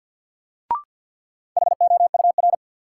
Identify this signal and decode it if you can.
Morse code, heard as a keyed tone, about 53 wpm, 700 Hz. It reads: HOUR
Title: Sets of 5 Words - New Word Drill - 50wpm